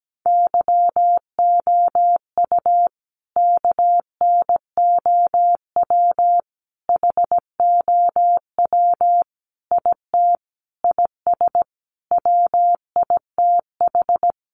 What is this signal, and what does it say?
Morse code, heard as a keyed tone, about 17 wpm, 700 Hz. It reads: YOU KNOW HOW IT IS WITH
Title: Sentences from Top 100 Words 17wpm